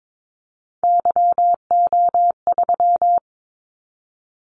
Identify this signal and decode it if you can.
Morse code, heard as a keyed tone, about 22 wpm, 700 Hz. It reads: YO3
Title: Callsign Prefixes for Top 25 DXCC - 22wpm